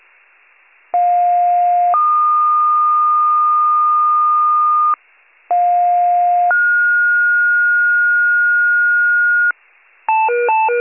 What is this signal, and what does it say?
Two-tone page, heard as a one-second tone, then a three-second tone, about 710 then 1220 Hz, about 710 then 1430 Hz, with noise. It is heard from 0.9 to 9.5 s.